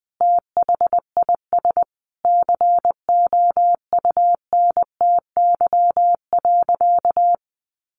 Morse code, heard as a keyed tone, about 20 wpm, 700 Hz. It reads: THIS COUNTY.
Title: Sentences from Top 1200 Words 20wpm